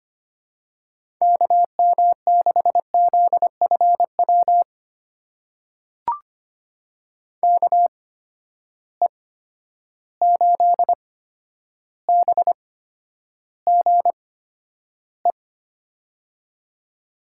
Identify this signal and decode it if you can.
Morse code, heard as a keyed tone, about 25 wpm, spaced as for 11 wpm, 700 Hz. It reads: KM6ZFW KE8BGE
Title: USA Call Signs - 6 Characters - 8x Character Spacing - 25wpm